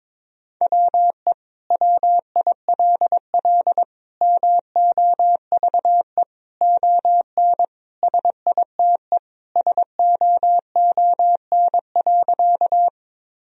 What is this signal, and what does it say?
Morse code, heard as a keyed tone, about 21 wpm, 700 Hz. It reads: WE WILL MOVE ON SITE SOON.
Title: Sentences from Top 1000 Words - Review Part 4 - 22wpm